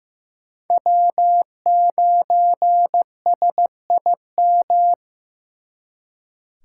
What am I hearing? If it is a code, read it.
Morse code, heard as a keyed tone, about 18 wpm, 700 Hz. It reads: W9SIM
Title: Speed Racing - USA 1x3 Call Signs 15wpm